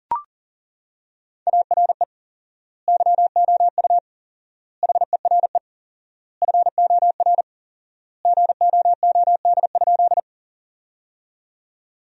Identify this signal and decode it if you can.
Morse code, heard as a keyed tone, about 40 wpm, 700 Hz. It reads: ARE YOU HERE FOR GOOD?
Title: 4x Word-Spacing - Sentences from Top 200 Words - 40wpm